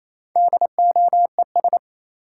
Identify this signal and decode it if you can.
Morse code, heard as a keyed tone, about 28 wpm, 700 Hz. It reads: DOES
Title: Sets of 2 Words - New Word Drill - 28wpm